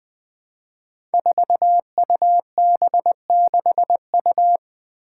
Morse code, heard as a keyed tone, about 20 wpm, 700 Hz. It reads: 4UB6U
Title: Five Challenging Letter-Number Combinations 20wpm